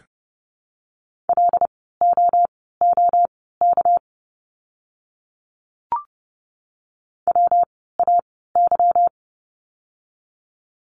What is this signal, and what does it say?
Morse code, heard as a keyed tone, about 30 wpm, spaced as for 17 wpm, 700 Hz. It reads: LOOK WAY
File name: course / 30:10wpm / 237 Lesson 38f - Letter q - Words 30wpm